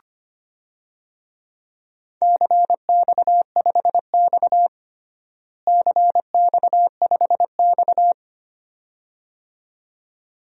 Morse code, heard as a keyed tone, about 25 wpm, 700 Hz. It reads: CX5X CX5X
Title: DXCC-100 Call Signs - 4 Characters - Repeated 2x - 25wpm